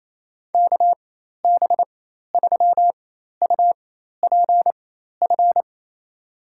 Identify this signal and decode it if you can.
Morse code, heard as a keyed tone, about 28 wpm, spaced as for 13 wpm, 700 Hz. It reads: KB3UPF
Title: USA Call Signs - 6 Characters - 4x Character Spacing - 28wpm